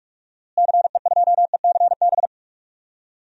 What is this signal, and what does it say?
Morse code, heard as a keyed tone, about 44 wpm, 700 Hz. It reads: CE2ECB
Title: DXCC-100 Call Signs - 6 Characters - Repeated 3x - 45wpm